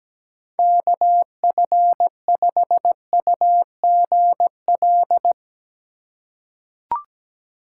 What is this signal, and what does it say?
Morse code, heard as a keyed tone, about 17 wpm, 700 Hz. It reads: KF5UGL
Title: USA Call Signs - 4x Character Spacing - 17wpm